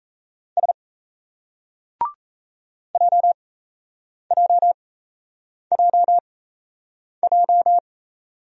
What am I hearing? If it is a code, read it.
Morse code, heard as a keyed tone, about 38 wpm, spaced as for 8 wpm, 700 Hz. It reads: S JJJJ